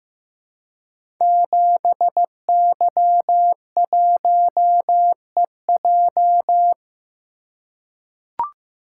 Morse code, heard as a keyed tone, about 15 wpm, 700 Hz. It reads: 7Y1EJ
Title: Code Groups - 5 Letter:Numbers - 15wpm